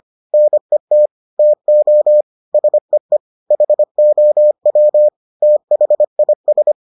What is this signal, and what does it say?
Morse code, heard as a keyed tone, about 25 wpm, 600 Hz. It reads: NET TO SEE HOW THIS